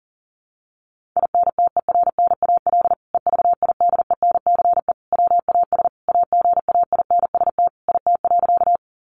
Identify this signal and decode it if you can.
Morse code, heard as a keyed tone, about 40 wpm, 700 Hz. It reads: INTERNAL EVIDENCE WAS AGAINST IT.
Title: Sentences from Top 1400 Words 40wpm